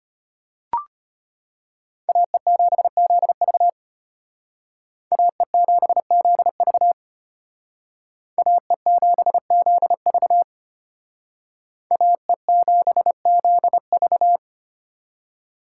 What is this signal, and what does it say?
Morse code, heard as a keyed tone, about 33 wpm, 700 Hz. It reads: AE7ZV AE7ZV AE7ZV AE7ZV